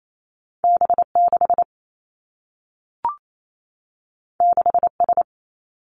Morse code, heard as a keyed tone, about 28 wpm, 700 Hz. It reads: B 6 6 S
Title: Two Challenging Letter-Number Combinations 28wpm